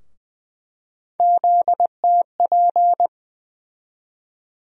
Morse code, heard as a keyed tone, about 20 wpm, 700 Hz. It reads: ZTP
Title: USA Call Sign Postfixes - 3 Letters 20wpm